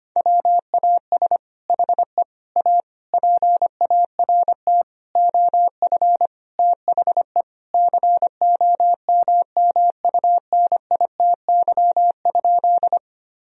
Morse code, heard as a keyed tone, about 25 wpm, 700 Hz. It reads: WAS HE A PART OF THE COMMUNITY?
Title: Sentences from Top 1000 Words - Review Part 4 - 25wpm